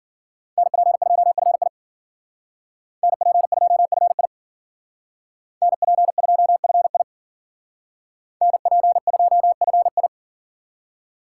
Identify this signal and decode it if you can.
Morse code, heard as a keyed tone, about 52 wpm, 700 Hz. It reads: NP2FI NP2FI NP2FI NP2FI